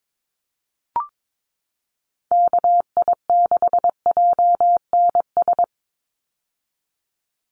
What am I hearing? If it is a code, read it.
Morse code, heard as a keyed tone, about 22 wpm, 700 Hz. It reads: KI6JNS
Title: USA Call Signs 22wpm